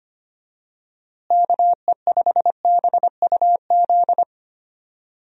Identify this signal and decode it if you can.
Morse code, heard as a keyed tone, about 25 wpm, 700 Hz. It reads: KE5BUZ
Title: USA Call Signs - 6 Characters - Repeated 2x - 25wpm